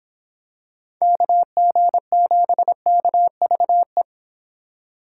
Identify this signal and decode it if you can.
Morse code, heard as a keyed tone, about 26 wpm, 700 Hz. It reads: KG7KVE